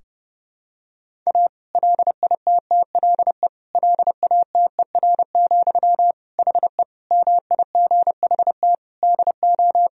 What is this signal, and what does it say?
Morse code, heard as a keyed tone, about 32 wpm, 700 Hz. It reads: A LITTLE LATER, HE MIGHT DO
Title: Sentences from Top 1000 Words - Review Part 4 - 30wpm